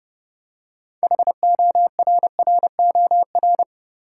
Morse code, heard as a keyed tone, about 30 wpm, 700 Hz. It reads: HORROR
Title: Six Letter Words - 30wpm